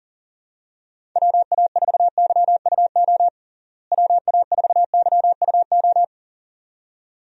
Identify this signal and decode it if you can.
Morse code, heard as a keyed tone, about 40 wpm, 700 Hz. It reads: WA4YUO WA4YUO
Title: USA Call Signs - 6 Characters - Repeated 2x - 40wpm